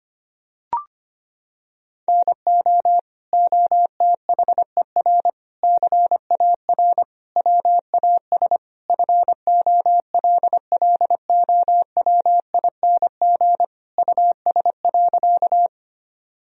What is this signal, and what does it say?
Morse code, heard as a keyed tone, about 25 wpm, 700 Hz. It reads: NO OTHER CAR WAS FOLLOWING US.
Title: Sentences from Top 1000 Words - Review Part 4 - 25wpm